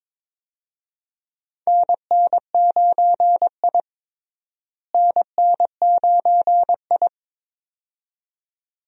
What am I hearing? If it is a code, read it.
Morse code, heard as a keyed tone, about 21 wpm, 700 Hz. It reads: NN9I NN9I
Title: USA Call Signs - 4 Characters - Repeated 2x - 22wpm